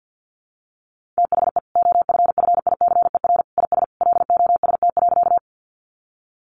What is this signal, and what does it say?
Morse code, heard as a keyed tone, about 50 wpm, 700 Hz. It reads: THE OFFICER IS LOST.
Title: Sentences from Top 1400 Words 50wpm